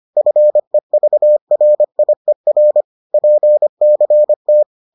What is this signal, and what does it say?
Morse code, heard as a keyed tone, about 25 wpm, 600 Hz. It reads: FEVRIER PCT
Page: bulletin UFT sous forme texte et en CW